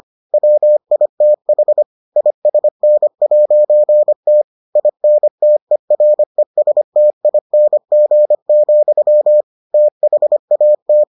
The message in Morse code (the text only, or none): WITH ISN'T INTERESTING, THAT